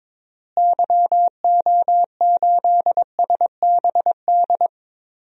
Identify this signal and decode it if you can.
Morse code, heard as a keyed tone, about 22 wpm, 700 Hz. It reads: YO8SBD
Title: DXCC-100 Call Signs - 6 Characters - Repeated 3x - 22wpm